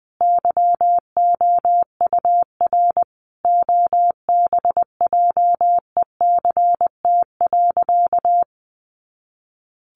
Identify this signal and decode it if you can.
Morse code, heard as a keyed tone, about 19 wpm, 700 Hz. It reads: YOUR OBJECT.
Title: Sentences from Top 1300 Words 20wpm